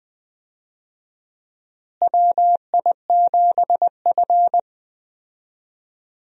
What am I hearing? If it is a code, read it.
Morse code, heard as a keyed tone, about 20 wpm, 700 Hz. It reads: WI7F
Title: USA Call Signs - 4 Characters - Repeated 3x - 20wpm